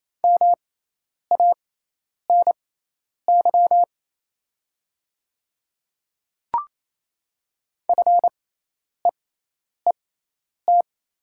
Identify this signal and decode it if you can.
Morse code, heard as a keyed tone, about 28 wpm, spaced as for 10 wpm, 700 Hz. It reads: MANY FEET
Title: 6x Character Spacing - Top 500 Words - 28wpm